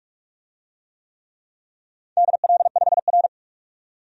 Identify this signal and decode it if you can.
Morse code, heard as a keyed tone, about 42 wpm, 700 Hz. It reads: DL5R
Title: DXCC-100 Call Signs - 4 Characters - 45wpm